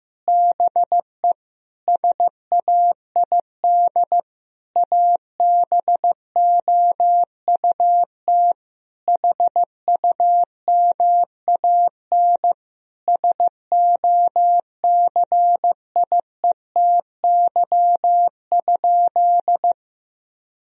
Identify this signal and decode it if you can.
Morse code, heard as a keyed tone, about 15 wpm, 700 Hz. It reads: BE SAID ABOUT HUMAN SOCIETY?
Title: Sentences from Top 1000 Words - Review Part 4 - 15wpm